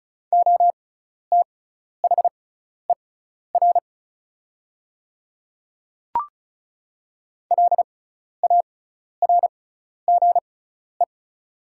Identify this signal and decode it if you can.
Morse code, heard as a keyed tone, about 36 wpm, spaced as for 12 wpm, 700 Hz. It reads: OTHER LARGE
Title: 6x Character Spacing - Top 200 Words - 35wpm